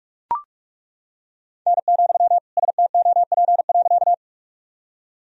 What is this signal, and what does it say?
Morse code, heard as a keyed tone, about 41 wpm, 700 Hz. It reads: N, STOP.
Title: Sentences from Top 300 Words and Abbreviations 45wpm